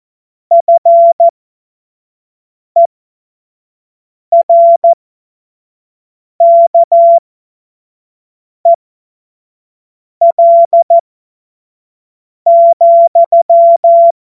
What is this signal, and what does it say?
Morse code, heard as a keyed tone, about 14 wpm, spaced as for 5 wpm, 680 Hz. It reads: FERKEL,
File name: Morsekurs / Witze / Witz 21. September 2021, 25 F70